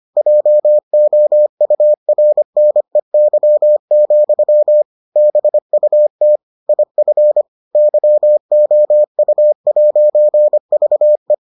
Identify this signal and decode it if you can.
Morse code, heard as a keyed tone, about 25 wpm, 600 Hz. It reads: JOURNEY, BUT IF YOU'VE